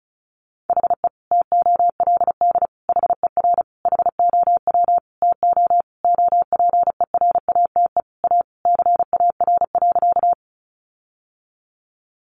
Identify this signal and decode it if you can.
Morse code, heard as a keyed tone, about 34 wpm, 700 Hz. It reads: HE TOLD HER HOW TO OPERATE A CAR.
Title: Sentences from Top 1500 Words 35wpm